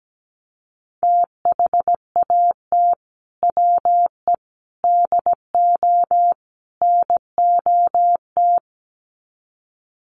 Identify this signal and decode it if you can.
Morse code, heard as a keyed tone, about 17 wpm, 700 Hz. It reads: THAT WE DO NOT
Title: Sets of 4 Words 17wpm